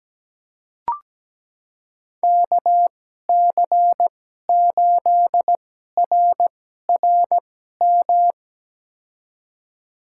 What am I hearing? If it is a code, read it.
Morse code, heard as a keyed tone, about 17 wpm, spaced as for 12 wpm, 700 Hz. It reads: KC8RRM
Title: USA Call Signs - 2x Character Spacing - 17wpm